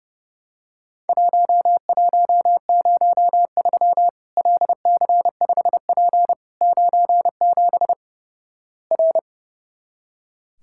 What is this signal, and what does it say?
Morse code, heard as a keyed tone, about 30 wpm, 700 Hz. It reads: 1103 LC5P 97 R
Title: Worked All Europe DX Contest - QTC Exchange - 30wpm